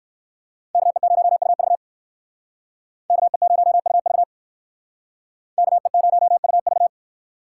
Morse code, heard as a keyed tone, about 58 wpm, 690 Hz. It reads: KE0UV KE0UV KE0UV